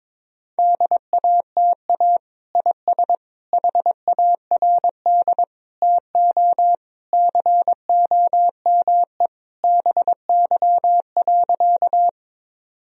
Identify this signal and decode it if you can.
Morse code, heard as a keyed tone, about 22 wpm, 700 Hz. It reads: DATA IS HARD TO COME BY.
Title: Sentences from Top 1000 Words - Review Part 3 - 22wpm